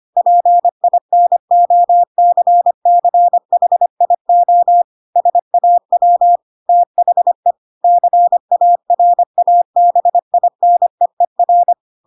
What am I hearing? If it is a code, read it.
Morse code, heard as a keyed tone, about 25 wpm, 700 Hz. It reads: PINOCCHIO SAW THE CARABINEER
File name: Pinocchio / Pinocchio ch3 25 25 ik1hge